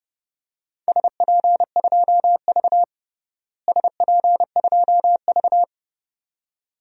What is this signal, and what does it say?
Morse code, heard as a keyed tone, about 30 wpm, 700 Hz. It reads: SP2V SP2V
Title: DXCC-100 Call Signs - 4 Characters - Repeated 2x - 30wpm